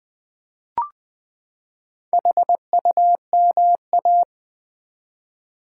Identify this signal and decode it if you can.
Morse code, heard as a keyed tone, about 20 wpm, 700 Hz. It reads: HUMA
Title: Build A Word - 4 Letters At A Time - 12-Letter Words - 20wpm